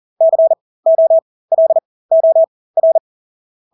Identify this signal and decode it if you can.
Morse code, heard as a keyed tone, about 40 wpm, spaced as for 20 wpm, 650 Hz. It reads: COLOR
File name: Engelske ord 40-20 wpm